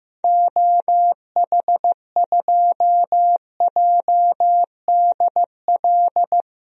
Morse code, heard as a keyed tone, about 15 wpm, 700 Hz. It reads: OH2JDL
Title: DXCC-100 Call Signs - 6 Characters - Repeated 2x - 15wpm